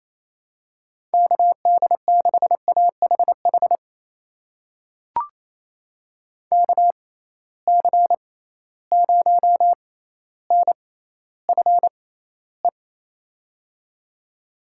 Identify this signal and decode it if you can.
Morse code, heard as a keyed tone, about 28 wpm, spaced as for 14 wpm, 700 Hz. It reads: KD6AHH KC0NFE
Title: USA Call Signs - 6x Character Spacing - 28wpm